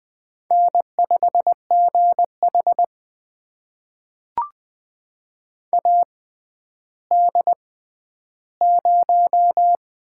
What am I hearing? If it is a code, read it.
Morse code, heard as a keyed tone, about 20 wpm, spaced as for 11 wpm, 700 Hz. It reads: N5GH AD0
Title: USA Call Signs - 4 Characters - 6x Character Spacing - 20wpm